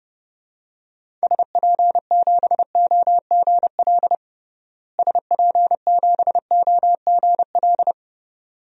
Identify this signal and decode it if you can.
Morse code, heard as a keyed tone, about 32 wpm, 700 Hz. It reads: SP7OGL SP7OGL
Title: DXCC-100 Call Signs - 6 Characters - Repeated 2x - 30wpm